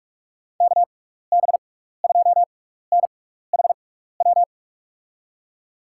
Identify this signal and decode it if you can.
Morse code, heard as a keyed tone, about 41 wpm, spaced as for 15 wpm, 700 Hz. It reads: KB2NHW
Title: USA Call Signs - 6 Characters - 6x Character Spacing - 45wpm